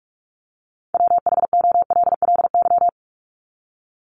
Morse code, heard as a keyed tone, about 47 wpm, 700 Hz. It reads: WHOLLY
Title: LY Suffix - Top 100 Words with Base Words 45wpm